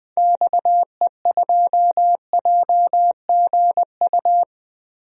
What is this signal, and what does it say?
Morse code, heard as a keyed tone, about 20 wpm, 700 Hz. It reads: XE2JGU
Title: DXCC-100 Call Signs - 6 Characters - Repeated 2x - 20wpm